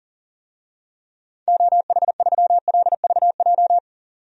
Morse code, heard as a keyed tone, about 40 wpm, 700 Hz. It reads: OH3LVJ
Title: DXCC-100 Call Signs - 6 Characters - 40wpm